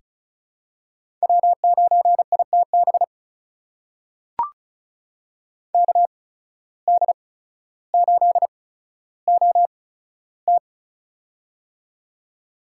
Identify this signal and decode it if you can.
Morse code, heard as a keyed tone, about 36 wpm, spaced as for 15 wpm, 700 Hz. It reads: W9ITB KD8OT
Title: USA Call Signs - 5 Characters - 8x Character Spacing - 35wpm